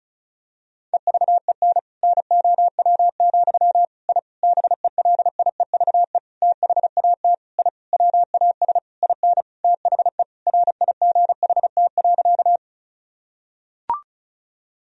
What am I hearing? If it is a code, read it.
Morse code, heard as a keyed tone, about 34 wpm, 700 Hz. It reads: EVEN NOW, I BELIEVE THAT I WAS IN THE RIGHT.
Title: Sentences from Top 1000 Words - Review Part 1 - 35wpm